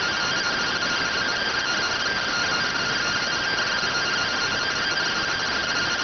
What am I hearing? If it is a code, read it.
Two-tone teleprinter signal, about 170 Hz shift.